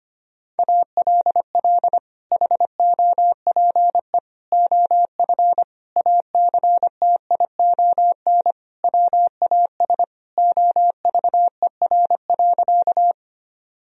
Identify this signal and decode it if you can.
Morse code, heard as a keyed tone, about 25 wpm, 700 Hz. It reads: ALL HOPE OF ACTION WAS OVER.
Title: Sentences from Top 1000 Words - Review Part 4 - 25wpm